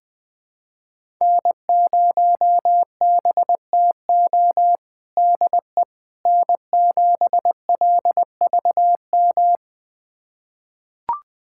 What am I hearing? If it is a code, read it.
Morse code, heard as a keyed tone, about 20 wpm, 700 Hz. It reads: N0BTO DE N7LVM
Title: USA Callsign DE Callsign - 5 Characters - 20wpm